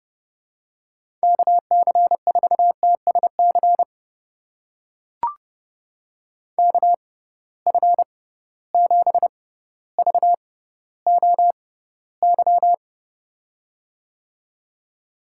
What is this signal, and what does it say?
Morse code, heard as a keyed tone, about 32 wpm, spaced as for 16 wpm, 700 Hz. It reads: KC4TSC KF7VOY
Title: USA Call Signs - 6 Characters - 6x Character Spacing - 30wpm